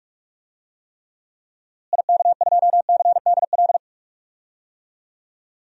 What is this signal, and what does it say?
Morse code, heard as a keyed tone, about 41 wpm, 700 Hz. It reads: IK2CDL
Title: DXCC-100 Call Signs - 6 Characters - Repeated 3x - 45wpm